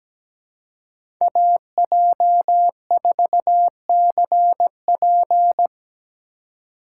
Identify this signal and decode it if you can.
Morse code, heard as a keyed tone, about 17 wpm, 700 Hz. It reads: AJ4CP
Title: USA Call Signs - 5 Characters - 3x Character Spacing - 17wpm